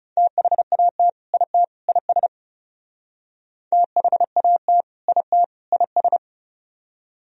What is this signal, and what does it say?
Morse code, heard as a keyed tone, about 37 wpm, 700 Hz. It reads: THAT IT IS THAT IT IS
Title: Speed-Racing - Sets of 3 Words 30wpm